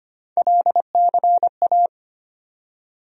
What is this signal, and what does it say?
Morse code, heard as a keyed tone, about 25 wpm, 700 Hz. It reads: LCA
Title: USA Call Sign Postfixes - 3 Letters 25wpm